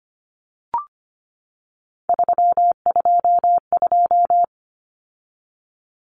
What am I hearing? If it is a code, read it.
Morse code, heard as a keyed tone, about 25 wpm, 700 Hz. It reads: 322
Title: Numbers - 3-Digit 25wpm